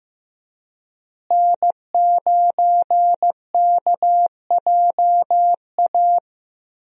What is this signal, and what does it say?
Morse code, heard as a keyed tone, about 15 wpm, 700 Hz. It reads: N9KJA
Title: USA Call Signs - 5 Characters - Repeated 2x - 15wpm